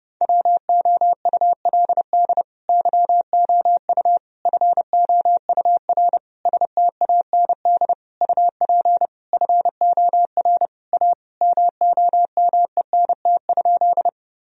Morse code, heard as a keyed tone, about 30 wpm, 700 Hz. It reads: WOULD YOU FOUR STAND UP FOR A MOMENT?
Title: Sentences from Top 1000 Words - Review Part 1 - 30wpm